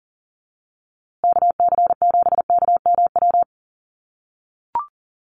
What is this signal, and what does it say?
Morse code, heard as a keyed tone, about 38 wpm, 700 Hz. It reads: KC7KMW
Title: USA Call Signs Repeated 3x 40wpm